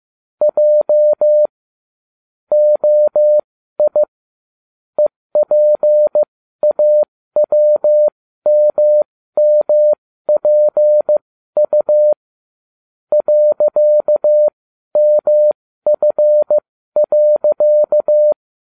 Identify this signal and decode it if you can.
Morse code, heard as a keyed tone, about 15 wpm, spaced as for 12 wpm, 600 Hz. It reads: J OI EPAWMMPU .MF.